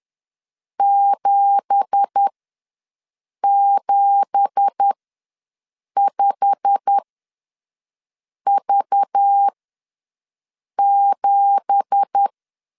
Morse code, heard as a keyed tone, about 10 wpm, spaced as for 5 wpm, 800 Hz. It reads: 775V7
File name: LEZIONI / 9 2867 ws 10wpm 4215